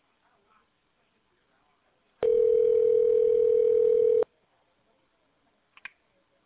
Ringback tone (North America), 440 + 480 Hz.